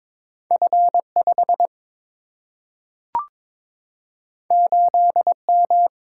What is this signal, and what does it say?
Morse code, heard as a keyed tone, about 22 wpm, 700 Hz. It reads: F 5 8 M
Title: Single-Letter Single-Number 22wpm